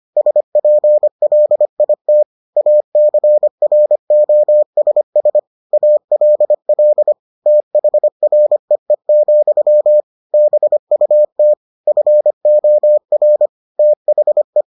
Morse code, heard as a keyed tone, about 25 wpm, 600 Hz. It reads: SPLIT ACROSS ALL THREE, BUT FOR THE